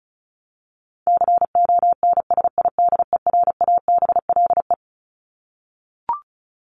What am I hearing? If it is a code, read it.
Morse code, heard as a keyed tone, about 34 wpm, 700 Hz. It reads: CONSIDERABLE
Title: LY Suffix - Top 100 Words with Base Words 35wpm